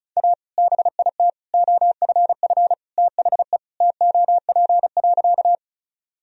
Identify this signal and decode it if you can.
Morse code, heard as a keyed tone, about 36 wpm, 700 Hz. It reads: A BIT OFF THE TOP.
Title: Sentences from Top 1000 Words - Review Part 2 - 35wpm